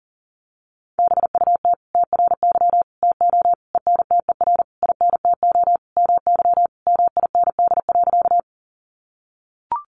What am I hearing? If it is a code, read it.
Morse code, heard as a keyed tone, about 40 wpm, 700 Hz. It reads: BUT TRY TO ENTER INTO MY MIND.
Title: Sentences from Top 1300 Words 40wpm